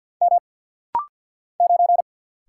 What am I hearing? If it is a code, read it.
Morse code, heard as a keyed tone, about 48 wpm, 700 Hz. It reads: M 9
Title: Single Letter-Number Alt - Rapid-Fire 50wpm